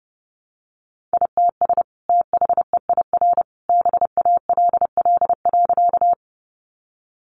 Morse code, heard as a keyed tone, about 30 wpm, 700 Hz. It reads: ITS THEIR BALL.
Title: Sentences from Top 900 Words 30wpm